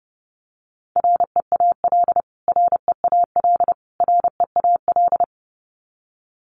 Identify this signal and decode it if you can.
Morse code, heard as a keyed tone, about 30 wpm, 700 Hz. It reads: REAL REAL REAL
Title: Top 500 Words Repeated 30wpm